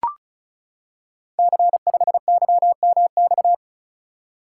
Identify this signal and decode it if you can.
Morse code, heard as a keyed tone, about 36 wpm, 700 Hz. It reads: C5YMX
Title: Code Groups - 5 Letter:Numbers - 35wpm